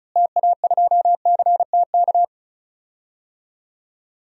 Morse code, heard as a keyed tone, about 36 wpm, 700 Hz. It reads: TA2CTK
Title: DXCC-100 Call Signs - 6 Characters - Repeated 3x - 35wpm